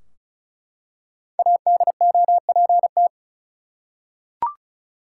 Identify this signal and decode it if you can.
Morse code, heard as a keyed tone, about 36 wpm, 700 Hz. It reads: ADOPT
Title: Five Letter Words - 35wpm